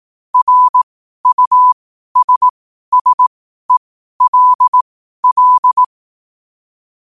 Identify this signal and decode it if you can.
Morse code, heard as a keyed tone, about 18 wpm, spaced as for 12 wpm, 1000 Hz. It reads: RUSSELL
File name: qotdfiles / 12wpm